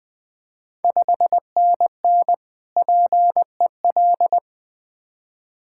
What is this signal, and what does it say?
Morse code, heard as a keyed tone, about 20 wpm, 690 Hz. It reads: 5NN PEL
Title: Ontario QSO Party Exchange 20wpm